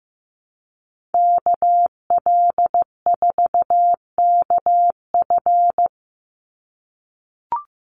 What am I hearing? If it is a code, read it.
Morse code, heard as a keyed tone, about 15 wpm, 700 Hz. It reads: KL4KF
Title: USA Call Signs 15wpm